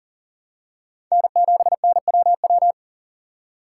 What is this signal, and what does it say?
Morse code, heard as a keyed tone, about 40 wpm, 700 Hz. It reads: N7NWW
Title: USA Call Signs - 5 Characters - 3x Character Spacing - 40wpm